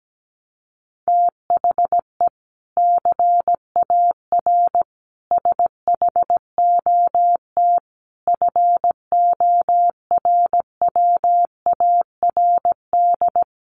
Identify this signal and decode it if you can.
Morse code, heard as a keyed tone, about 17 wpm, 700 Hz. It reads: THE CAR SHOT FORWARD
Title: Sentences from Top 1300 Words 17wpm